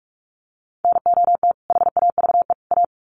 Morse code, heard as a keyed tone, about 44 wpm, 700 Hz. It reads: NOT HAVE A
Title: Sets of 3 Words 45wpm